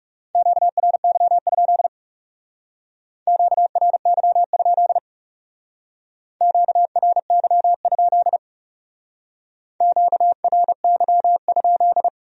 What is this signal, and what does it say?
Morse code, heard as a keyed tone, about 39 wpm, 700 Hz. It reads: QRY? QRY? QRY? QRY?